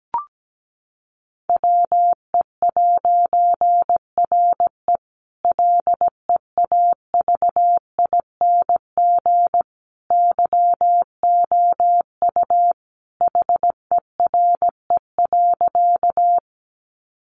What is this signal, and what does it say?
Morse code, heard as a keyed tone, about 17 wpm, 700 Hz. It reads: WE'RE LEAVING YOU HERE.
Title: A Sound Of Thunder 17wpm